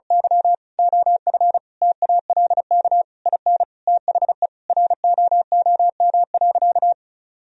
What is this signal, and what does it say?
Morse code, heard as a keyed tone, about 36 wpm, 700 Hz. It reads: Y OF TALK IN THE ROOM.
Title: Sentences from Top 1000 Words - Review Part 4 - 35wpm